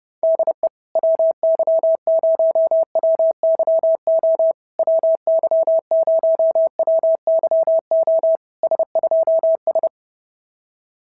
Morse code, heard as a keyed tone, about 27 wpm, 650 Hz. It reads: DE WY0WYO WY0WYO S2S
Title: SOTA QSO Practice 30wpm